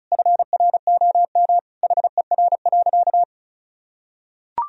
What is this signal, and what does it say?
Morse code, heard as a keyed tone, about 35 wpm, 700 Hz. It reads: FROM HER.
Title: Sentences from Top 1000 Words - Review Part 2 - 35wpm